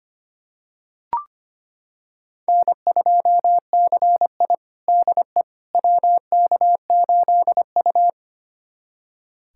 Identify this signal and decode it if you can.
Morse code, heard as a keyed tone, about 25 wpm, 700 Hz. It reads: N2CI DE WK8U
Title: USA Callsign DE Callsign - 4 Characters - 25wpm